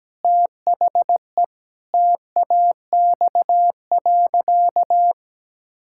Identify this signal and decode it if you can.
Morse code, heard as a keyed tone, about 17 wpm, 700 Hz. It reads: THE TAX.
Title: Sentences from Top 1000 Words - Review Part 5 - 17wpm